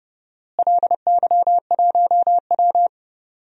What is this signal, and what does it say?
Morse code, heard as a keyed tone, about 32 wpm, 700 Hz. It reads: LY1W
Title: DXCC-100 Call Signs - 4 Characters - Repeated 3x - 30wpm